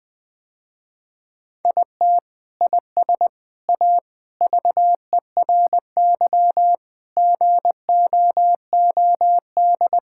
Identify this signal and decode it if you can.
Morse code, heard as a keyed tone, about 20 wpm, 700 Hz. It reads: IT IS A VERY GOOD
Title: Sentences with 3-Word Phrases - Spoken Beginning - 20wpm